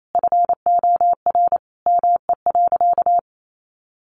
Morse code, heard as a keyed tone, about 28 wpm, 700 Hz. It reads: FOR ME.
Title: Sentences from Top 300 Words 28wpm